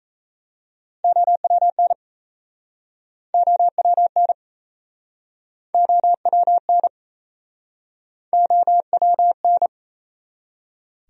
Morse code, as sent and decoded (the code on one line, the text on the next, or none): --- .-- -. / --- .-- -. / --- .-- -. / --- .-- -.
OWN OWN OWN OWN